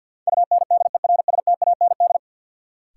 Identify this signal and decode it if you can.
Morse code, heard as a keyed tone, about 53 wpm, 700 Hz. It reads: UNDERSTAND
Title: Sets of 4 Words - New Word Drill - 50wpm